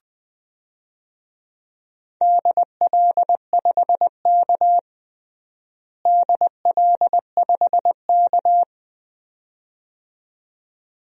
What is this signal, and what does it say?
Morse code, heard as a keyed tone, about 20 wpm, 700 Hz. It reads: DL5K DL5K
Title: DXCC-100 Call Signs - 4 Characters - Repeated 2x - 20wpm